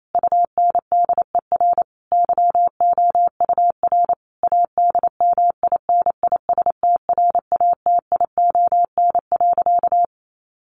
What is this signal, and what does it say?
Morse code, heard as a keyed tone, about 28 wpm, 700 Hz. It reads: UNDER YOUR ADMINISTRATION.
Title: Sentences from Top 1500 Words 28wpm